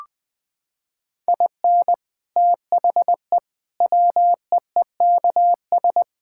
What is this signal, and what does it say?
Morse code, heard as a keyed tone, about 20 wpm, 700 Hz. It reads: IN THE WEEKS
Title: Sets of 6 Words - 20wpm